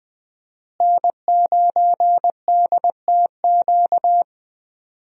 Morse code, heard as a keyed tone, about 20 wpm, 700 Hz. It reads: N9DTQ
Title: USA Call Signs - 5 Characters - Repeated 3x - 20wpm